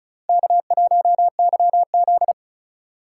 Morse code, heard as a keyed tone, about 46 wpm, 700 Hz. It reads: K1YZ